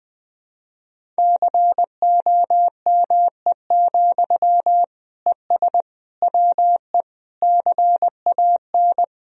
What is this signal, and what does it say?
Morse code, heard as a keyed tone, about 20 wpm, 700 Hz. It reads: COME, ES WE CAN
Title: Sentences from Top 500 Words and Abbreviations 20wpm